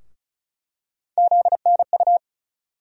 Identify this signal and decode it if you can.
Morse code, heard as a keyed tone, about 36 wpm, 700 Hz. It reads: ZNU
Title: USA Call Sign Postfixes - 3 Letters 35wpm